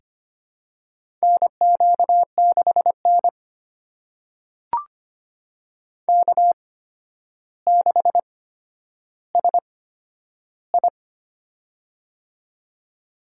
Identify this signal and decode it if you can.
Morse code, heard as a keyed tone, about 25 wpm, spaced as for 11 wpm, 700 Hz. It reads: NQ6N K6SI
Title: USA Call Signs - 8x Character Spacing - 25wpm